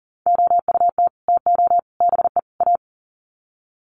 Morse code, heard as a keyed tone, about 40 wpm, 700 Hz. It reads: OUT TO BE A
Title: Sets of 4 Words 40wpm